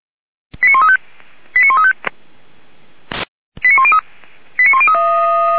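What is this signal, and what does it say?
Five-tone selective call (ZVEI), heard as quick runs of five tones, tones of about 70 ms. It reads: 89135 89135 89103 89103
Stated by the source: ZVEI five-tone selective call 89227 received by SDR, from feuerwehr-orlamuende.org fire brigade recordings.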